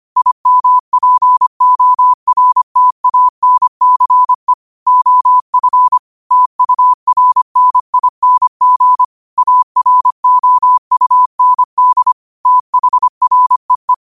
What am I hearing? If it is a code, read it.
Morse code, heard as a keyed tone, about 25 wpm, 1000 Hz. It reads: IMPORTANCE OF TURNING AROUND THREE